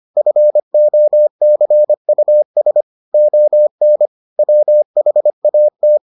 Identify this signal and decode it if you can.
Morse code, heard as a keyed tone, about 25 wpm, 600 Hz. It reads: FOCUS ON WHAT